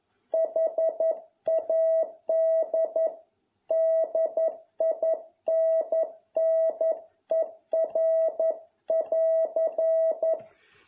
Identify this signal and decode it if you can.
Morse code, heard as a keyed tone, about 11 wpm, 650 Hz. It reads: HAD DINNER<AR>